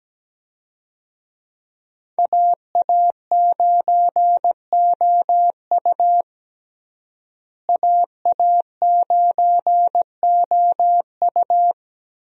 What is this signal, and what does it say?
Morse code, heard as a keyed tone, about 17 wpm, 700 Hz. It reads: AA9OU AA9OU
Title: USA Call Signs - 5 Characters - Repeated 3x - 17wpm